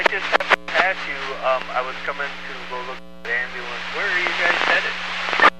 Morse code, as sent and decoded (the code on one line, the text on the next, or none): none